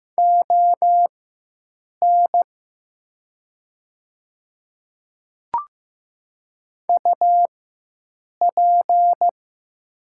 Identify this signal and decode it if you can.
Morse code, heard as a keyed tone, about 15 wpm, spaced as for 7 wpm, 700 Hz. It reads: ON UP
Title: 4x Character Spacing - Top 100 Words - 15wpm